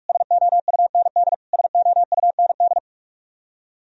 Morse code, heard as a keyed tone, about 46 wpm, 700 Hz. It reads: SOUND SOUND
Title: Top 100 Words Repeated 45wpm